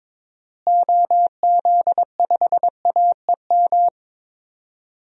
Morse code, heard as a keyed tone, about 22 wpm, 700 Hz. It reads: OZ5AEM
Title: DXCC-100 Call Signs - 6 Characters - Repeated 3x - 22wpm